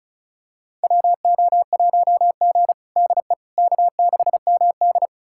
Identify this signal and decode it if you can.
Morse code, heard as a keyed tone, about 36 wpm, 700 Hz. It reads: WO1G DE K6MD
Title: USA Callsign DE Callsign - 4 Characters - 35wpm